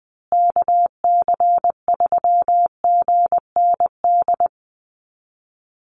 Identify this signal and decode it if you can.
Morse code, heard as a keyed tone, about 20 wpm, 700 Hz. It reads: KC3GND
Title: USA Call Signs Repeated 2x 20wpm